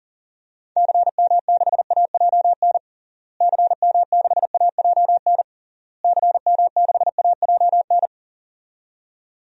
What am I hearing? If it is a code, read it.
Morse code, heard as a keyed tone, about 37 wpm, 700 Hz. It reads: CM6AJN CM6AJN CM6AJN
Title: DXCC-100 Call Signs - 6 Characters - Repeated 3x - 40wpm